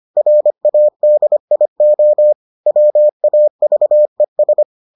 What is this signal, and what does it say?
Morse code, heard as a keyed tone, about 25 wpm, 600 Hz. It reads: RADIO WAVES